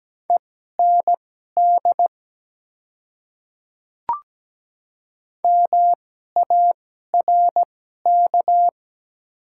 Morse code, heard as a keyed tone, about 17 wpm, spaced as for 12 wpm, 690 Hz. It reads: END MARK
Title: 2x Character Spacing - Top 300 Words - 17wpm